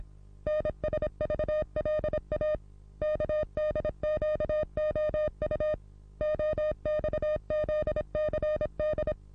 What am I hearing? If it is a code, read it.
Morse code, heard as a keyed tone, about 26 wpm, 610 Hz. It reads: NSVLA KDQOU OXZCD
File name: repos / Morsekurs / DJ4UF Morsen27